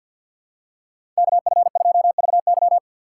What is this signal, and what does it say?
Morse code, heard as a keyed tone, about 47 wpm, 700 Hz. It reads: KF2VY
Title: USA Call Signs - 3x Character Spacing - 50wpm